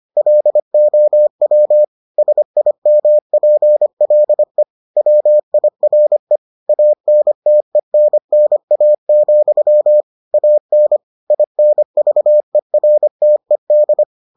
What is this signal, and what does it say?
Morse code, heard as a keyed tone, about 25 wpm, 600 Hz. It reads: LOW SIMPLE WIRE ANTENNA, AN INVERTED